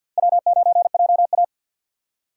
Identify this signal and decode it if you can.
Morse code, heard as a keyed tone, about 53 wpm, 700 Hz. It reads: W9JA
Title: USA Call Signs - 4 Characters - 6x Character Spacing - 50wpm